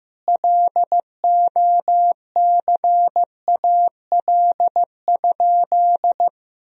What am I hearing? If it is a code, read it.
Morse code, heard as a keyed tone, about 16 wpm, 700 Hz. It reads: LOCAL?
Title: Sentences from Top 1000 Words - Review Part 3 - 15wpm